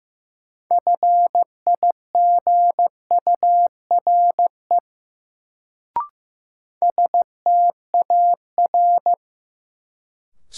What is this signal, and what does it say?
Morse code, heard as a keyed tone, about 16 wpm, 700 Hz. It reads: FIGURE STAR
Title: Sentences from Top 1000 Words - Review Part 2 - New Word Drill - 15wpm